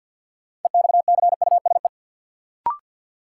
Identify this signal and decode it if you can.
Morse code, heard as a keyed tone, about 49 wpm, 700 Hz. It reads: EXCUSE
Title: Sentences from Top 700 Words - New Word Drill - 50wpm